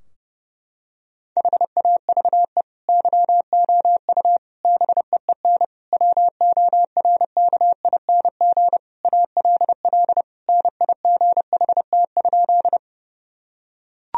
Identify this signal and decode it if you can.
Morse code, heard as a keyed tone, about 32 wpm, 700 Hz. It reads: HAVE YOU BEEN WORKING ALL NIGHT?
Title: Sentences from Top 1000 Words - Review Part 4 - 30wpm